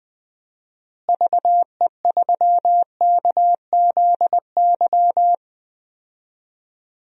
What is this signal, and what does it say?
Morse code, heard as a keyed tone, about 21 wpm, 700 Hz. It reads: VE3KZY
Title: DXCC-100 Call Signs - 6 Characters - Repeated 2x - 20wpm